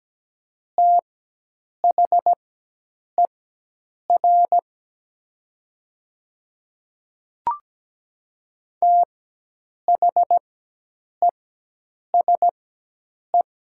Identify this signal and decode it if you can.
Morse code, heard as a keyed tone, about 17 wpm, spaced as for 8 wpm, 700 Hz. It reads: THER THESE
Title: 4x Character Spacing - Top 100 Words - 17wpm